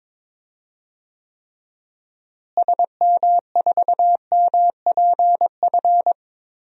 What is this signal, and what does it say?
Morse code, heard as a keyed tone, about 22 wpm, 700 Hz. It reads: SM4MPF
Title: DXCC-100 Call Signs - 6 Characters - Repeated 2x - 22wpm